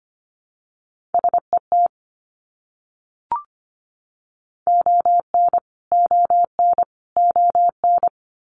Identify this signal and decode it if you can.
Morse code, heard as a keyed tone, about 25 wpm, 700 Hz. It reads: SET ON ON ON
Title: Top 300 Words Repeated 25wpm